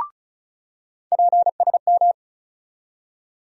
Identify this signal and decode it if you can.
Morse code, heard as a keyed tone, about 36 wpm, 700 Hz. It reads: PSM
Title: USA Call Sign Postfixes - 3 Letters 35wpm